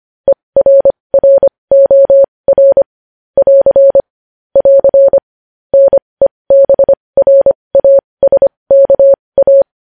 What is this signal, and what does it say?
Morse code, heard as a keyed tone, about 25 wpm, spaced as for 20 wpm, 550 Hz. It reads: ERROR <AR> <AR> NEBRASKA